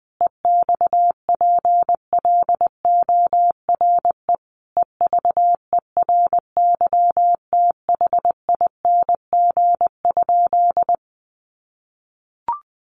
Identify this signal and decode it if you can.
Morse code, heard as a keyed tone, about 20 wpm, 700 Hz. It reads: EXPLORE EVERYTHING?
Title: Sentences from Top 1100 Words 20wpm